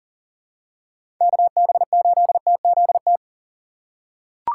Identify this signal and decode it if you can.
Morse code, heard as a keyed tone, about 38 wpm, 700 Hz. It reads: KB8TZT
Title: USA Call Signs - 4x Character Spacing - 40wpm